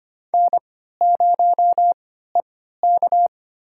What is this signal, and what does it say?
Morse code, heard as a keyed tone, about 25 wpm, spaced as for 14 wpm, 700 Hz. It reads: N0EK
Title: USA Call Signs - 4 Characters - 3x Character Spacing - 25wpm